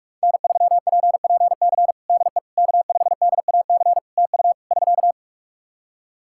Morse code, heard as a keyed tone, about 45 wpm, 700 Hz. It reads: N3PPC DE K5DAC TU <SK>
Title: K1USN SST QSO Practice 45wpm